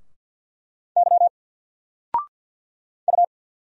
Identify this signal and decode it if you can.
Morse code, heard as a keyed tone, about 48 wpm, 700 Hz. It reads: Y U
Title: Single Letter-Number Alt 50wpm